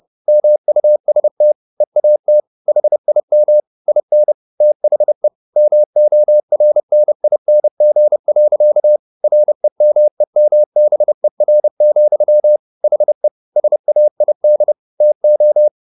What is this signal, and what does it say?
Morse code, heard as a keyed tone, about 30 wpm, 600 Hz. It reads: MUST EAT HIM IN THE MORNING. REMEMBER, HE SAID TO
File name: the old man and the sea / the old man and the sea 1 30wpm